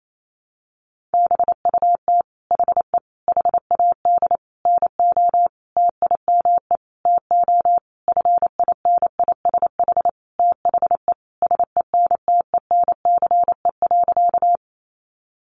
Morse code, heard as a keyed tone, about 28 wpm, 700 Hz. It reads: BUT HE HAD NO TIME TO FINISH THE SENTENCE.
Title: Sentences from Top 1200 Words 28wpm